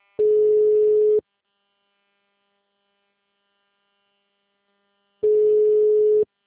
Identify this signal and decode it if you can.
Ringback tone (Europe, 425 Hz).
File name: out- -2021- - - .18